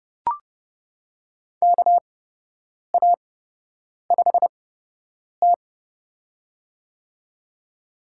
Morse code, heard as a keyed tone, about 29 wpm, spaced as for 8 wpm, 700 Hz. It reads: KA5T
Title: USA Call Signs - 4 Characters - 8x Character Spacing - 30wpm